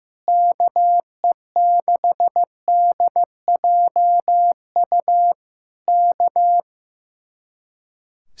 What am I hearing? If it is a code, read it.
Morse code, heard as a keyed tone, about 15 wpm, 700 Hz. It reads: KE6DJU K
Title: Speed-Racing Common QSO Phrases Encore - 15wpm